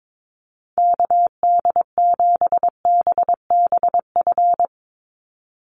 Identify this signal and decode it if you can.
Morse code, heard as a keyed tone, about 22 wpm, 700 Hz. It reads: KD7BBF
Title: USA Call Signs 22wpm